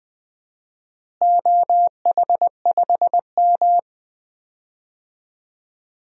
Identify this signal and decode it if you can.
Morse code, heard as a keyed tone, about 20 wpm, 700 Hz. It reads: OH5M
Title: DXCC-100 Call Signs - 4 Characters - Repeated 3x - 20wpm